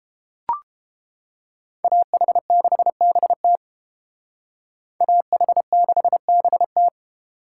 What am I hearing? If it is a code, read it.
Morse code, heard as a keyed tone, about 29 wpm, 700 Hz. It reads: AH6BT AH6BT